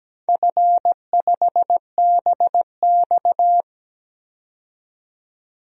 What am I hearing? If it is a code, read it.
Morse code, heard as a keyed tone, about 17 wpm, 700 Hz. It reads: F5BX
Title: DXCC-100 Call Signs - 4 Characters - 17wpm